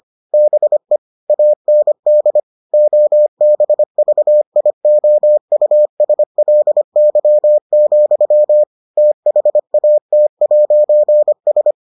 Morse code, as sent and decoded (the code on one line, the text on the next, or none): -... / .- -. -.. / --- -... ...- .. --- ..- ... .-.. -.-- --..-- / - .... .- - .----. ...
BE AND OBVIOUSLY, THAT'S